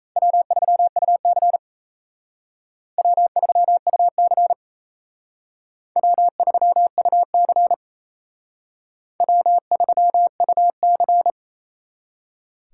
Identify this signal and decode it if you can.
Morse code, heard as a keyed tone, about 35 wpm, 700 Hz. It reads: W3UC W3UC W3UC W3UC
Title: Speed Racing - USA Call Signs 28wpm